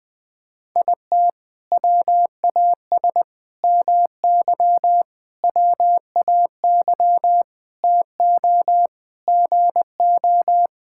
Morse code, heard as a keyed tone, about 19 wpm, 700 Hz. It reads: IT WAS MY WAY TO GO
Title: Sentences from Top 1600 Words 20wpm